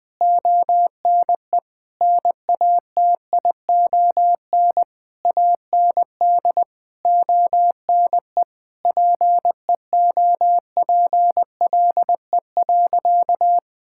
Morse code, heard as a keyed tone, about 20 wpm, 700 Hz. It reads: ONE NATION AND ONE PEOPLE.
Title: Sentences from Top 1000 Words - Review Part 6 - 20wpm